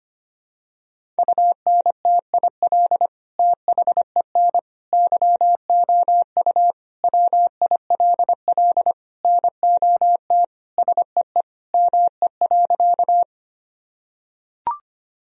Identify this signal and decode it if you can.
Morse code, heard as a keyed tone, about 25 wpm, 700 Hz. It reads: UNTIL THEN YOU WILL NOT SEE ME.
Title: Sentences from Top 1000 Words - Review Part 1 - 25wpm